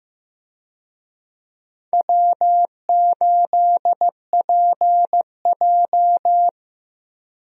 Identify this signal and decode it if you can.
Morse code, heard as a keyed tone, about 15 wpm, 700 Hz. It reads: W8PJ
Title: USA Call Signs - 4 Characters - Repeated 3x - 15wpm